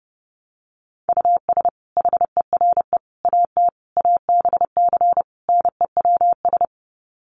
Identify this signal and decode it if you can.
Morse code, heard as a keyed tone, about 30 wpm, 700 Hz. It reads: US HERE AT ABC NEWS
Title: Sets of 5 Words 30wpm